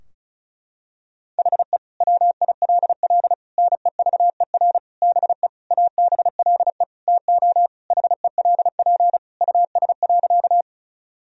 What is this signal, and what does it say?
Morse code, heard as a keyed tone, about 34 wpm, 700 Hz. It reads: HE WILL NEVER BE ABLE TO HELP US.
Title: Sentences from Top 1000 Words - Review Part 2 - 35wpm